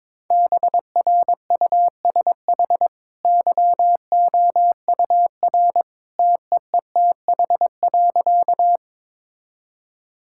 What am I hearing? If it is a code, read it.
Morse code, heard as a keyed tone, about 22 wpm, 690 Hz. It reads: BRUSH YOUR TEETH.
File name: practice / 22wpm / Easy Sentences with Hints - 22wpm